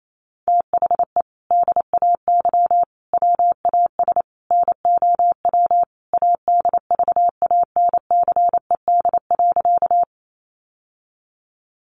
Morse code, heard as a keyed tone, about 28 wpm, 700 Hz. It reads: THE DAY WAS NOW ADVANCED.
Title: Sentences from Top 1200 Words 28wpm